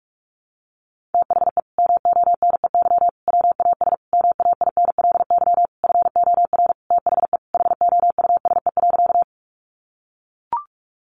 Morse code, heard as a keyed tone, about 44 wpm, 700 Hz. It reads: THE MONEY WAS MAINLY FOR THE HOUSE.
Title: Sentences from Top 1400 Words 45wpm